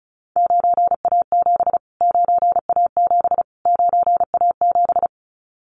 Morse code, heard as a keyed tone, about 34 wpm, 700 Hz. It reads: 9A7 9A7 9A7
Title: Callsign Prefixes for Top 25 DXCC - Repeated 3x - 35wpm